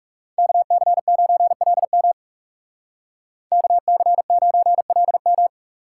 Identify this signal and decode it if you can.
Morse code, heard as a keyed tone, about 34 wpm, 700 Hz. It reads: KC9LM KC9LM
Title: Speed Racing - USA 2x2 Call Signs 30wpm